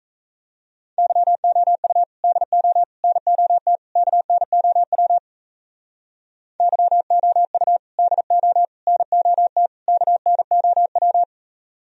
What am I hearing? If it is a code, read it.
Morse code, heard as a keyed tone, about 36 wpm, 700 Hz. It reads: YOU DO NOT KNOW YOU DO NOT KNOW